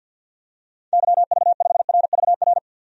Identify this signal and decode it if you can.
Morse code, heard as a keyed tone, about 51 wpm, 700 Hz. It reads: YV5RVR
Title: DXCC-100 Call Signs - 6 Characters - 50wpm